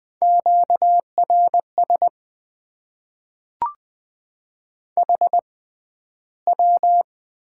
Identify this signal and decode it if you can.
Morse code, heard as a keyed tone, about 20 wpm, spaced as for 12 wpm, 700 Hz. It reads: QRS HW
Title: Top 90 QSO Elements - 6x Character Spacing - 20wpm